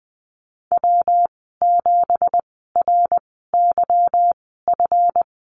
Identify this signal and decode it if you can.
Morse code, heard as a keyed tone, about 20 wpm, spaced as for 15 wpm, 700 Hz. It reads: W7RYF
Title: 186 Lesson 30g - Letter b - Call Signs 20wpm